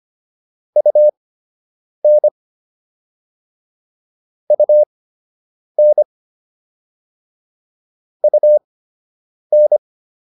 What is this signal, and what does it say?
Morse code, heard as a keyed tone, about 25 wpm, spaced as for 8 wpm, 600 Hz. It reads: UN UN UN